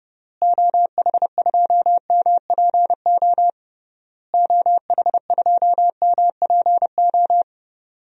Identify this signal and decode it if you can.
Morse code, heard as a keyed tone, about 30 wpm, 700 Hz. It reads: OH2MPO OH2MPO
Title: DXCC-100 Call Signs - 6 Characters - Repeated 3x - 30wpm